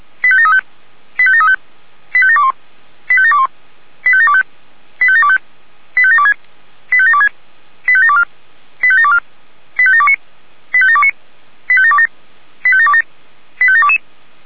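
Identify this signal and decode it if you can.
Five-tone selective call (ZVEI), heard as quick runs of five tones, tones of about 70 ms. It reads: 85735 85735 85721 85721 85725 85725 85726 85726 85724 85724 85729 85729 85727 85727 85722
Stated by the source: ZVEI five-tone selective call 85825, received by SDR.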